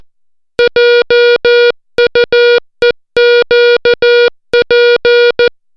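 Morse code, heard as a keyed tone, about 14 wpm, 480 Hz. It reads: JUEQP
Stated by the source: Morse alphabet lesson, 60 characters per minute.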